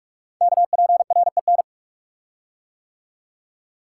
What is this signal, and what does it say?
Morse code, heard as a keyed tone, about 44 wpm, 690 Hz. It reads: KPREN
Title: Code Groups - 5 Letters - 45wpm